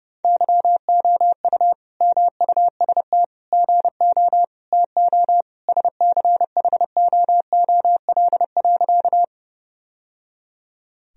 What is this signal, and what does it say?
Morse code, heard as a keyed tone, about 30 wpm, 700 Hz. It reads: YOU MUST GO TO SCHOOL.
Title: Sentences from Top 1000 Words - Review Part 1 - 30wpm